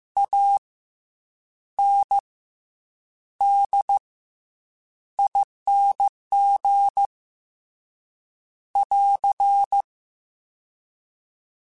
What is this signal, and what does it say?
Morse code, heard as a keyed tone, about 15 wpm, 800 Hz. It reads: A N D ING <AR>
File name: Special INGs / Special INGs-mp3s / SlowToFaster ing @13